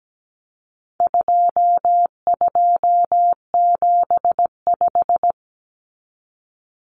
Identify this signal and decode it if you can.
Morse code, heard as a keyed tone, about 17 wpm, 700 Hz. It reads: 2275
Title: Common Cut Numbers - 4-Digit 17wpm